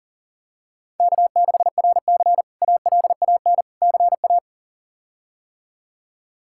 Morse code, heard as a keyed tone, about 40 wpm, 700 Hz. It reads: K6RC ALAN CA
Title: K1USN SST QSO Practice - 40wpm